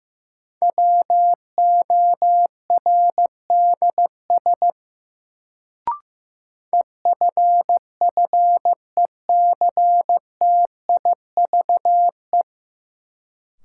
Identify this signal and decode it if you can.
Morse code, heard as a keyed tone, about 15 wpm, 700 Hz. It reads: WORDS EFFECTIVE
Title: Sentences from Top 1000 Words - Review Part 6 - New Word Drill - 15wpm